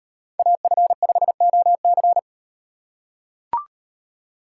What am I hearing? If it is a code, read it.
Morse code, heard as a keyed tone, about 36 wpm, 700 Hz. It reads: AF5OC